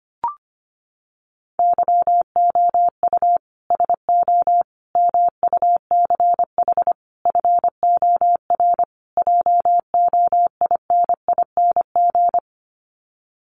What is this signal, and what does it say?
Morse code, heard as a keyed tone, about 25 wpm, 700 Hz. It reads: YOU SO MUCH FOR JOINING
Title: Sets of 5 Words Encore2 25wpm